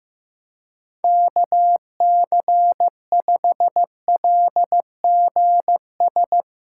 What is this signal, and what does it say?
Morse code, heard as a keyed tone, about 15 wpm, 700 Hz. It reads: KC5LGS
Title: Speed Racing - USA 2x3 Call Signs 15wpm